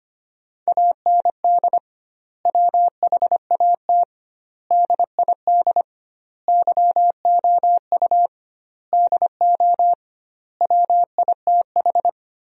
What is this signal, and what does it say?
Morse code, heard as a keyed tone, about 25 wpm, 700 Hz. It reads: AND WHAT DID YOU DO WITH
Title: 2x Word-Spacing - Sentences from Top 300 Words - 25wpm